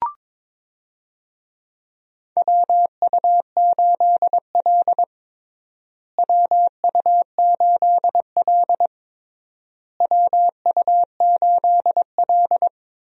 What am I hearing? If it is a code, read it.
Morse code, heard as a keyed tone, about 21 wpm, 700 Hz. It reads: WU8L WU8L WU8L
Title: USA Call Signs - 4 Characters - Repeated 3x - 22wpm